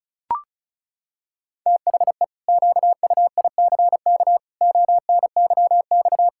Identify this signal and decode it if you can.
Morse code, heard as a keyed tone, about 34 wpm, 700 Hz. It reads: THE QUICK ONYX
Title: Pangram Challenge v2 - 35wpm